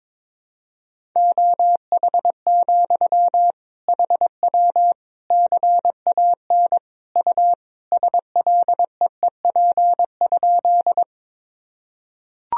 Morse code, heard as a keyed tone, about 22 wpm, 700 Hz. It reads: OH, HW CAN U SLEEP?
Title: Sentences from Top 500 Words and Abbreviations 22wpm